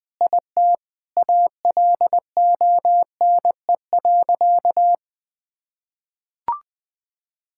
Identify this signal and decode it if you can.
Morse code, heard as a keyed tone, about 20 wpm, 700 Hz. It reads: IT ALONE.
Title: Sentences with 6-Word Phrases - 20wpm